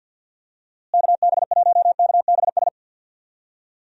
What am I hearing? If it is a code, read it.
Morse code, heard as a keyed tone, about 52 wpm, 700 Hz. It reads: KB1KBS
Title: USA Call Signs - 6 Characters - Repeated 3x - 50wpm